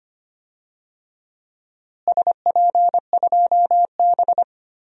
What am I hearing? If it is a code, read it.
Morse code, heard as a keyed tone, about 25 wpm, 700 Hz. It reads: SP2B
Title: DXCC-100 Call Signs - 4 Characters - Repeated 2x - 25wpm